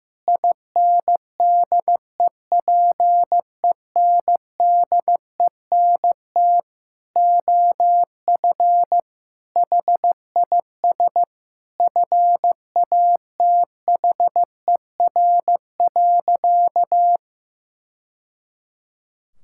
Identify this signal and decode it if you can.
Morse code, heard as a keyed tone, about 15 wpm, 700 Hz. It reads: INDEPENDENT OF HIS FATHER.
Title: Sentences from Top 1000 Words - Review Part 6 - 15wpm